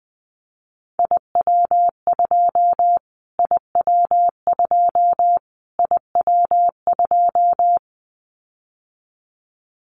Morse code, heard as a keyed tone, about 20 wpm, 700 Hz. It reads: IW2 IW2 IW2
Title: Callsign Prefixes for Top 25 DXCC - Repeated 3x - 20wpm